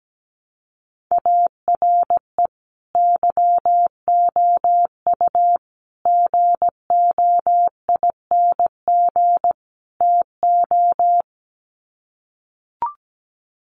Morse code, heard as a keyed tone, about 17 wpm, 700 Hz. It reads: ARE YOU GOING TO
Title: Sets of 4 Words 17wpm